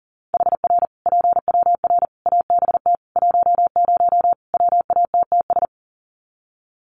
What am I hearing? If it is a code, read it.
Morse code, heard as a keyed tone, about 40 wpm, 700 Hz. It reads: HR PWR ABT 10 WATTS
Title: Common QSO Phrases 40wpm